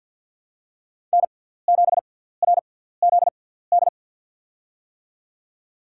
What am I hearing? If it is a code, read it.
Morse code, heard as a keyed tone, about 54 wpm, spaced as for 17 wpm, 700 Hz. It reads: N7RZD
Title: USA Call Signs - 6x Character Spacing - 50wpm